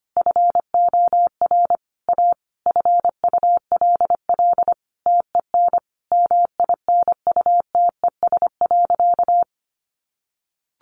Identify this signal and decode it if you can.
Morse code, heard as a keyed tone, about 25 wpm, 700 Hz. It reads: FOR A FULL TEN MINUTES.
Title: A Sound Of Thunder 25wpm